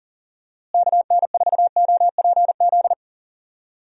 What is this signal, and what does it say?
Morse code, heard as a keyed tone, about 38 wpm, 700 Hz. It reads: KN4OPZ
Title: USA Call Signs - 6 Characters - Repeated 2x - 40wpm